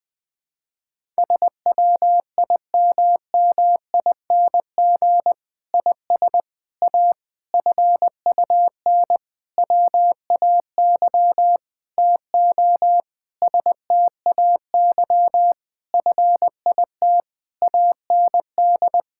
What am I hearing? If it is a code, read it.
Morse code, heard as a keyed tone, about 20 wpm, 700 Hz. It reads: SWIMMING IS A FUN WAY TO STAY FIT AND
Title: Difficult Sentences with Hints - 20wpm